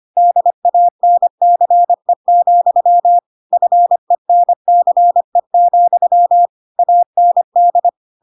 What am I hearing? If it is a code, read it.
Morse code, heard as a keyed tone, about 25 wpm, 700 Hz. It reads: DANCE, FENCE, AND
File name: Pinocchio / Pinocchio ch2 25 25 ik1hge